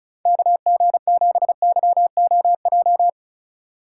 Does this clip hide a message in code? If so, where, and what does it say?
Yes: Morse code from 0.3 to 3.1 s, heard as a keyed tone, about 44 wpm, 700 Hz. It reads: KG7YOJ